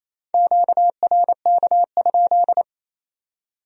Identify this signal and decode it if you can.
Morse code, heard as a keyed tone, about 35 wpm, 700 Hz. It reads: QRK?
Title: Speed-Racing Q-Codes - 28wpm